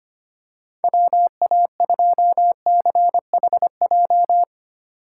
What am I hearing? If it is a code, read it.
Morse code, heard as a keyed tone, about 25 wpm, 700 Hz. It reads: WA2CHJ
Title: USA Call Signs - 6 Characters - 3x Character Spacing - 25wpm